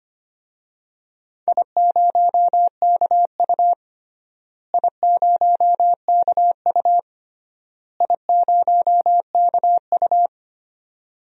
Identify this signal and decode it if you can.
Morse code, heard as a keyed tone, about 25 wpm, 700 Hz. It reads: I0KU I0KU I0KU
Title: DXCC-100 Call Signs - 4 Characters - Repeated 3x - 25wpm